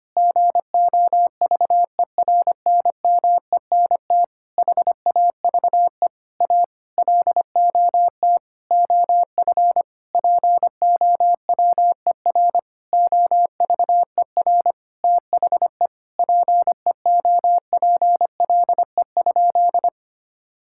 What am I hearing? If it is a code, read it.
Morse code, heard as a keyed tone, about 25 wpm, 700 Hz. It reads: GOVERNMENT HAVE A LOT OF POWER OVER THE PEOPLE?
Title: Sentences from Top 1000 Words - Review Part 5 - 25wpm